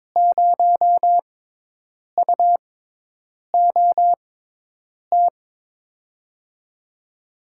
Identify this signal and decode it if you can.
Morse code, heard as a keyed tone, about 22 wpm, spaced as for 8 wpm, 700 Hz. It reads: 0UOT
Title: USA Call Signs - 6 Characters - 6x Character Spacing - 22wpm